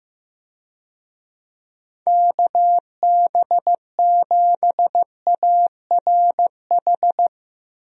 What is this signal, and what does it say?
Morse code, heard as a keyed tone, about 15 wpm, 700 Hz. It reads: KB7ARH
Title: USA Call Signs - 6 Characters - Repeated 3x - 15wpm